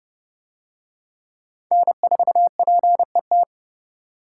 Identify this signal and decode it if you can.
Morse code, heard as a keyed tone, about 26 wpm, 700 Hz. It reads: N4PET